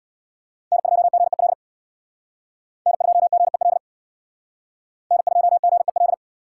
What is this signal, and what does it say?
Morse code, heard as a keyed tone, about 59 wpm, 700 Hz. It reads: N2GEL N2GEL N2GEL
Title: Speed Racing - USA 1x3 Call Signs 50wpm